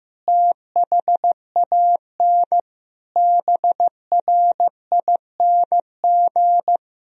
Morse code, heard as a keyed tone, about 15 wpm, 700 Hz. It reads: THAN BRING
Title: Sentences from Top 1000 Words - Review Part 3 - 15wpm